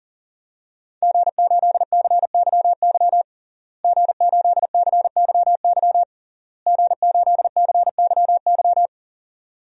Morse code, heard as a keyed tone, about 40 wpm, 700 Hz. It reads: G8CYY G8CYY G8CYY
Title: DXCC-100 Call Signs - 5 Characters - Repeated 3x - 40wpm